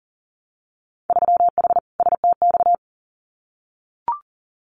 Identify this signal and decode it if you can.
Morse code, heard as a keyed tone, about 40 wpm, 700 Hz. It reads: 3H STX
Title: Winter Field Day Exchanges Encore 40wpm